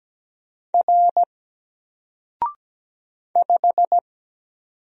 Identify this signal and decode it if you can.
Morse code, heard as a keyed tone, about 17 wpm, 700 Hz. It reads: R 5
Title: Single Letter-Number Alt 17wpm